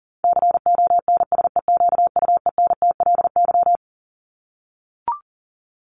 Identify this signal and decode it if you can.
Morse code, heard as a keyed tone, about 40 wpm, 700 Hz. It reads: CONSEQUENTLY
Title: LY Suffix - Top 100 Words with Base Words 40wpm